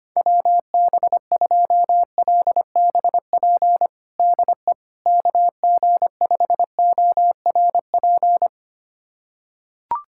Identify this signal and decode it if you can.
Morse code, heard as a keyed tone, about 25 wpm, 700 Hz. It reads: WB2LBP DE KG5ORP
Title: USA Callsign DE Callsign - 6 Characters - 25wpm